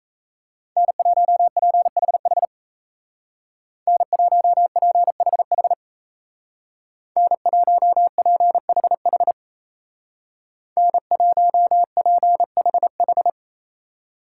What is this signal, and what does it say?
Morse code, heard as a keyed tone, about 35 wpm, 700 Hz. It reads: N1PHH N1PHH N1PHH N1PHH
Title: Speed Racing - USA Call Signs 28wpm